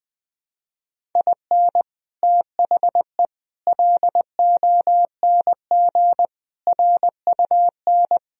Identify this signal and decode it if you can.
Morse code, heard as a keyed tone, about 20 wpm, 700 Hz. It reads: IN THE LONG RUN